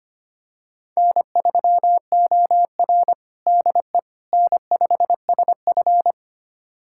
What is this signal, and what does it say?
Morse code, heard as a keyed tone, about 25 wpm, 700 Hz. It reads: N3OR DE N5SF
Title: USA Callsign DE Callsign - 4 Characters - 25wpm